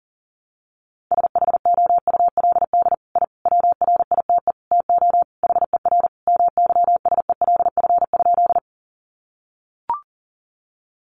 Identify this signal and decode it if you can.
Morse code, heard as a keyed tone, about 40 wpm, 700 Hz. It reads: SHOULD I WRITE TO HER MYSELF?
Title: Sentences from Top 1300 Words 40wpm